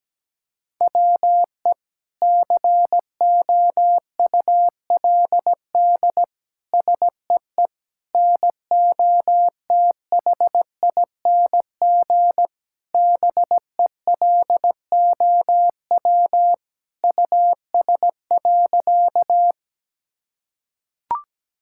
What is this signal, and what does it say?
Morse code, heard as a keyed tone, about 17 wpm, 700 Hz. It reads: WE COULD SEE NOTHING BELOW US.
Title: Sentences from Top 1000 Words - Review Part 5 - 17wpm